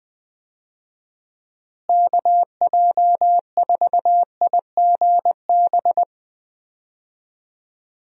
Morse code, heard as a keyed tone, about 20 wpm, 700 Hz. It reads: KJ4IGB
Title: USA Call Signs - 6 Characters - Repeated 3x - 20wpm